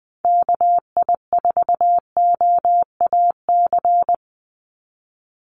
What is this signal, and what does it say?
Morse code, heard as a keyed tone, about 20 wpm, 700 Hz. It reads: KI4OAC
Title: USA Call Signs 20wpm